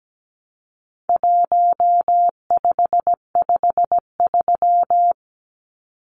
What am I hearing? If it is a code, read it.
Morse code, heard as a keyed tone, about 17 wpm, 700 Hz. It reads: 1553
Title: Numbers - 4-Digit 17wpm